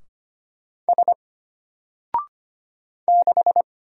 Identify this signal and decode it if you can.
Morse code, heard as a keyed tone, about 25 wpm, 700 Hz. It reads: S 6
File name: Single-Letter-Number-25wpm